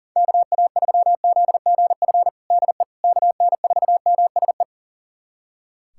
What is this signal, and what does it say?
Morse code, heard as a keyed tone, about 40 wpm, 700 Hz. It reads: KA3ZGF DE KN4MSE
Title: USA Callsign DE Callsign 40wpm